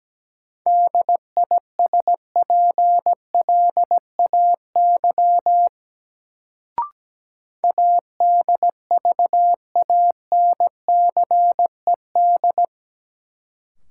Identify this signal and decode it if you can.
Morse code, heard as a keyed tone, about 17 wpm, 700 Hz. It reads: DISPLAY ADVANCED
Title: Sentences from Top 1200 Words - New Word Drill - 17wpm